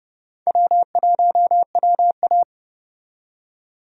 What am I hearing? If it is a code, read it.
Morse code, heard as a keyed tone, about 30 wpm, 700 Hz. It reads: W1WA
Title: USA Call Signs - 4 Characters - 30wpm